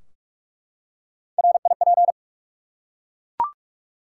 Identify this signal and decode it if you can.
Morse code, heard as a keyed tone, about 44 wpm, 700 Hz. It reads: RIP
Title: Three Letter Words - 45wpm